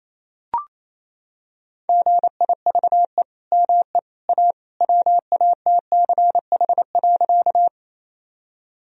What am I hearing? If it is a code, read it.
Morse code, heard as a keyed tone, about 28 wpm, 700 Hz. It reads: GIVE ME A WATCH.
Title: Sentences from Top 1000 Words - Review Part 1 - 28wpm